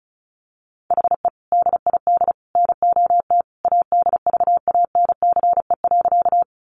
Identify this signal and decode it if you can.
Morse code, heard as a keyed tone, about 34 wpm, 700 Hz. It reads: HE DID NOT ADVANCE.
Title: Sentences from Top 1400 Words 35wpm